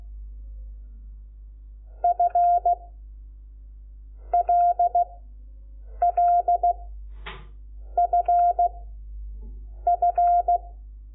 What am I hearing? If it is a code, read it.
Morse code, heard as a keyed tone, about 16 wpm, 700 Hz. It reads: F L L F F
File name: Lecciones / CLASE Nº5